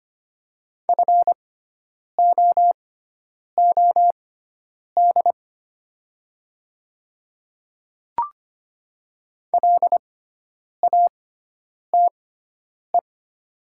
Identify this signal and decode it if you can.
Morse code, heard as a keyed tone, about 25 wpm, spaced as for 9 wpm, 700 Hz. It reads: FOOD LATE
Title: 6x Character Spacing - Top 300 Words - 25wpm